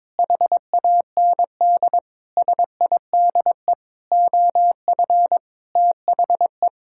Morse code, heard as a keyed tone, about 22 wpm, 700 Hz. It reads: HAND SIDE OF THE
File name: Sentences with 7-Word Phrases - 22wpm